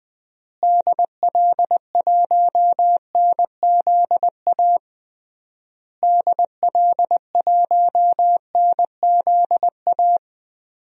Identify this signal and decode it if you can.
Morse code, heard as a keyed tone, about 20 wpm, 700 Hz. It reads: DL1NZA DL1NZA
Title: DXCC-100 Call Signs - 6 Characters - Repeated 3x - 20wpm